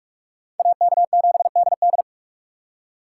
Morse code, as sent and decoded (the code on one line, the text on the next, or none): .- -.- --... -.. -..
AK7DD